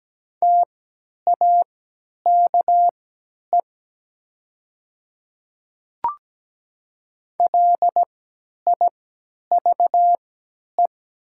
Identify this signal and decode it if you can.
Morse code, heard as a keyed tone, about 17 wpm, spaced as for 10 wpm, 700 Hz. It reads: TAKE LIVE
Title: 3x Character Spacing - Top 300 Words - 17wpm